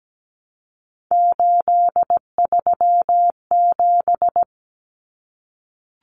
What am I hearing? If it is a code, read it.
Morse code, heard as a keyed tone, about 17 wpm, 700 Hz. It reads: 837
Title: Numbers - 3-Digit 17wpm